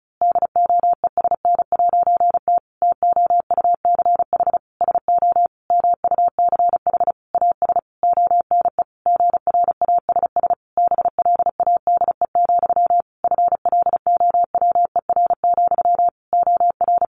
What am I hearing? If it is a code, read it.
Morse code, heard as a keyed tone, about 34 wpm, 700 Hz. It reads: DOESN'T TOUCH SO MUCH AS ONE GRASS BLADE, FLOWER, OR
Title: A Sound Of Thunder 35wpm